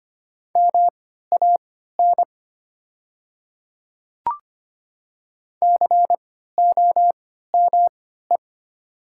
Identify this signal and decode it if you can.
Morse code, heard as a keyed tone, about 25 wpm, spaced as for 14 wpm, 700 Hz. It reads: MAN COME
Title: 3x Character Spacing - Top 300 Words - 25wpm